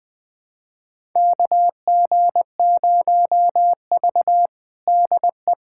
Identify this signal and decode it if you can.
Morse code, heard as a keyed tone, about 21 wpm, 700 Hz. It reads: KG0V DE
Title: USA Callsign DE Callsign - 4 Characters - 20wpm